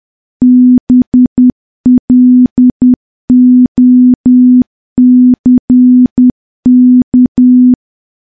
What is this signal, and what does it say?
Morse code, heard as a keyed tone, about 10 wpm, 250 Hz. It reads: BLOCK
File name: morse / morse-training-10WPM